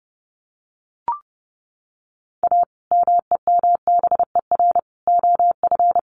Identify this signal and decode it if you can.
Morse code, heard as a keyed tone, about 30 wpm, 700 Hz. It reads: A MEMBER OF
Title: Sets of 3 Words 30wpm